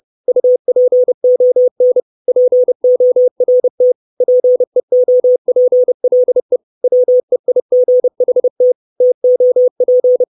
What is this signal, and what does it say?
Morse code, heard as a keyed tone, about 30 wpm, 500 Hz. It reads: UPON PORT PEOPLE WEIGHT TOP
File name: 632 Words 30WPM